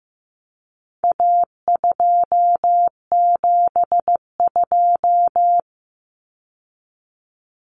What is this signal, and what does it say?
Morse code, heard as a keyed tone, about 15 wpm, 700 Hz. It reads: A272
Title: Common Cut Numbers - 4-Digit 15wpm